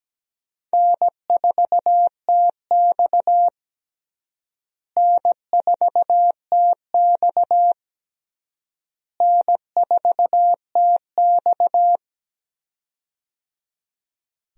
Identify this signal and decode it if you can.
Morse code, heard as a keyed tone, about 17 wpm, 700 Hz. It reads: N4TX N4TX N4TX
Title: USA Call Signs - 4 Characters - Repeated 3x - 17wpm